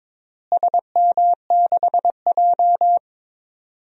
Morse code, heard as a keyed tone, about 22 wpm, 700 Hz. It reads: SM6J
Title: DXCC-100 Call Signs - 4 Characters - Repeated 3x - 22wpm